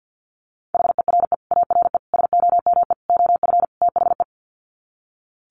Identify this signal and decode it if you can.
Morse code, heard as a keyed tone, about 48 wpm, 700 Hz. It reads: HERE ARE SOME OF THE
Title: Sets of 5 Words Encore 50wpm